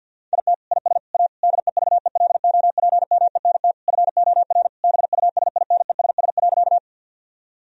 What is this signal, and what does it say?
Morse code, heard as a keyed tone, about 51 wpm, 700 Hz. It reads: IT IS A DEVELOPMENT FOR BUSINESS.
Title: Sentences from Top 1000 Words - Review Part 4 - 50wpm